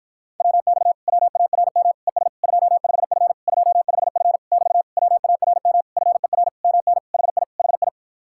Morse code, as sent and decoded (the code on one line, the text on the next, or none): .-- -..- / .-- .- .-. -- / . ... / ..--- ..... ..-. / ..--- ..... ..-. / -...- / .-- .- .-. -- / ..-. .-. / -- -. / .... .. / .... ..
WX WARM ES 25F 25F <BT> WARM FER MN HI HI